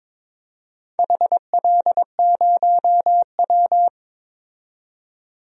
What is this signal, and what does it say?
Morse code, heard as a keyed tone, about 22 wpm, 700 Hz. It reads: HL0W
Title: DXCC-100 Call Signs - 4 Characters - 22wpm